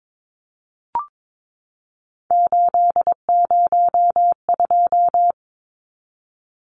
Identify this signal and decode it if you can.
Morse code, heard as a keyed tone, about 22 wpm, 700 Hz. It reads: 802
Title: Numbers - 3-Digit 22wpm